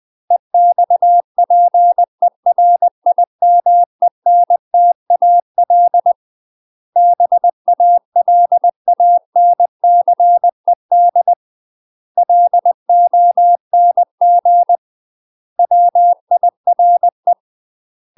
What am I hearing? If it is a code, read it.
Morse code, heard as a keyed tone, about 20 wpm, 700 Hz. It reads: EXPERIMENTAL BALANCED LONG WIRE
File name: MASTER QSOs TRAINING/20WPM